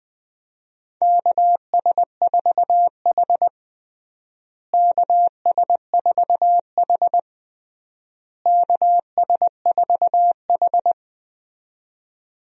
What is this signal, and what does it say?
Morse code, heard as a keyed tone, about 20 wpm, 700 Hz. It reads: KS4H KS4H KS4H
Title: USA Call Signs - 4 Characters - Repeated 3x - 20wpm